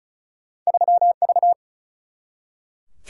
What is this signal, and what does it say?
Morse code, heard as a keyed tone, about 34 wpm, 700 Hz. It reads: 3V
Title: Single-Letter Single-Number 35wpm